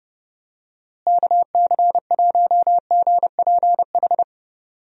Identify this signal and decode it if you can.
Morse code, heard as a keyed tone, about 32 wpm, 700 Hz. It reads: KC1GPH
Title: USA Call Signs - 6 Characters - Repeated 3x - 30wpm